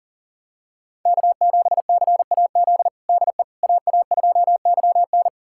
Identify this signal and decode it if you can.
Morse code, heard as a keyed tone, about 40 wpm, 700 Hz. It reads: K7CAZ DE AA2YN
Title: USA Callsign DE Callsign - 5 Characters - 40wpm